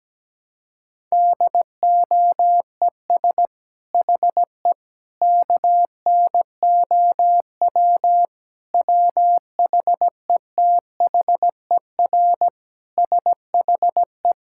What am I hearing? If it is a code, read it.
Morse code, heard as a keyed tone, about 17 wpm, 700 Hz. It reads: DOES HE KNOW WHETHER SHE
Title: Sentences from Top 1000 Words - Review Part 6 - 17wpm